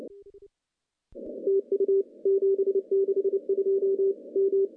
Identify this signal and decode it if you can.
Morse code, heard as a keyed tone, about 27 wpm, 390 Hz. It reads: TU 762 M
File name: contests / CW-SS-k5tr- -0700